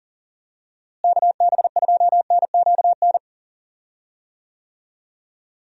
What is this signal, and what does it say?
Morse code, heard as a keyed tone, about 38 wpm, 700 Hz. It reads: KB2NQN
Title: USA Call Signs - 6 Characters - Repeated 2x - 40wpm